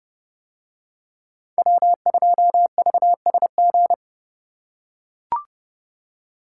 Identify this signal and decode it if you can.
Morse code, heard as a keyed tone, about 29 wpm, 700 Hz. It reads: W2VSG